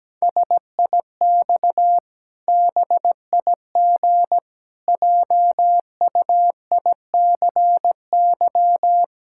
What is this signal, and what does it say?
Morse code, heard as a keyed tone, about 17 wpm, 700 Hz. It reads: SIX BIG JUICY
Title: Pangram Challenge v2 - 17wpm